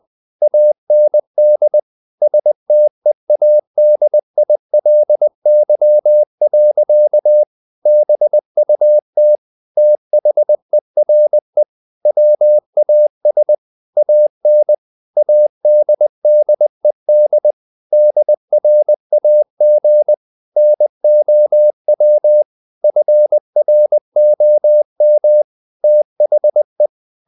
Morse code, heard as a keyed tone, about 20 wpm, 600 Hz. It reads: AND STEADILY. BUT THERE WAS AN ADDED DRAG NOW FROM THE
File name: the old man and the sea / the old man and the sea part 1 20wpm